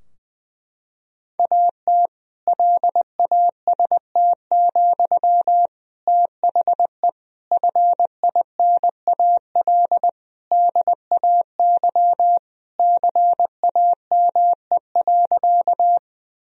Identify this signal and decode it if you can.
Morse code, heard as a keyed tone, about 21 wpm, 700 Hz. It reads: AT LAST, THE FINAL DAY CAME.
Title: Sentences from Top 1000 Words - Review Part 2 - 20wpm